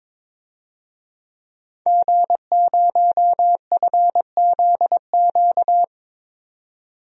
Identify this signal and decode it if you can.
Morse code, heard as a keyed tone, about 22 wpm, 700 Hz. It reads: G0FZQ
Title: DXCC-100 Call Signs - 5 Characters - 22wpm